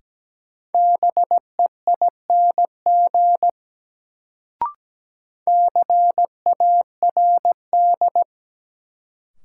Morse code, heard as a keyed tone, about 17 wpm, 700 Hz. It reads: BEING CARD
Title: Sentences from Top 600 Words - New Word Drill - 17wpm